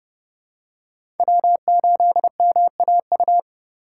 Morse code, heard as a keyed tone, about 32 wpm, 700 Hz. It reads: W8MAU
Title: USA Call Signs - 5 Characters - 4x Character Spacing - 30wpm